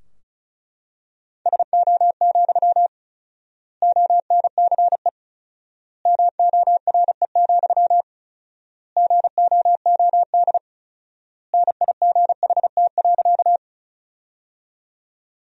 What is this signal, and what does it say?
Morse code, heard as a keyed tone, about 36 wpm, 700 Hz. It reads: SO, ONCE MORE, GOOD NIGHT.
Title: 4x Word-Spacing - Sentences from Top 300 Words - 35wpm